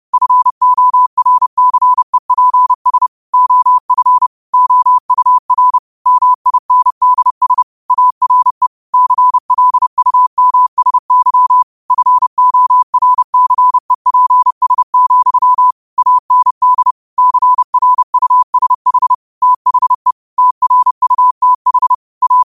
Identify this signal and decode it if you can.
Morse code, heard as a keyed tone, about 30 wpm, 1000 Hz. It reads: FORCEPS OF OUR MINDS ARE CLUMSY FORCEPS, AND CRUSH THE TRUTH A
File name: qotdfiles / 30wpm